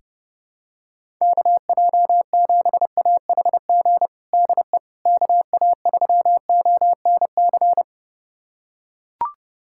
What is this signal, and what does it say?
Morse code, heard as a keyed tone, about 30 wpm, 700 Hz. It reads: KJ7AHG DE KA3ONC
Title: USA Callsign DE Callsign 30wpm